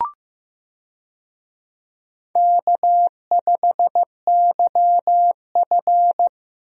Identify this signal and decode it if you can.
Morse code, heard as a keyed tone, about 16 wpm, 700 Hz. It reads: K5YF
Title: USA Call Signs - 4 Characters - Repeated 2x - 15wpm